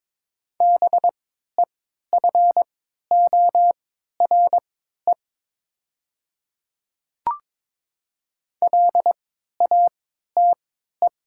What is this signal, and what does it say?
Morse code, heard as a keyed tone, about 22 wpm, spaced as for 12 wpm, 700 Hz. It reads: BEFORE LATE
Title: 3x Character Spacing - Top 300 Words - 22wpm